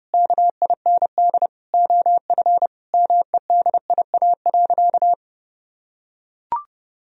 Morse code, heard as a keyed tone, about 32 wpm, 700 Hz. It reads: KIND OF MEDIA.
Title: Sentences from Top 1000 Words - Review Part 4 - 30wpm